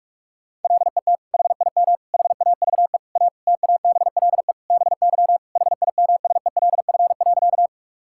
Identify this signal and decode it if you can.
Morse code, heard as a keyed tone, about 46 wpm, 700 Hz. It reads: LET HIM HAVE A TABLE BY HIMSELF.
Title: Sentences from Top 1600 Words 45wpm